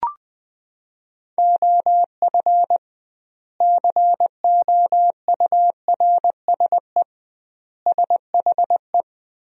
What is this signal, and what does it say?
Morse code, heard as a keyed tone, about 20 wpm, 700 Hz. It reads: OF COURSE SHE
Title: 2x Word-Spacing - Sentences from Top 500 Words - 20wpm